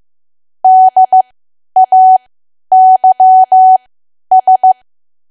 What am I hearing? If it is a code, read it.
Morse code, heard as a keyed tone, about 15 wpm, spaced as for 10 wpm, 750 Hz. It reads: DAYS
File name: Archive / 10 WPM / 10WPM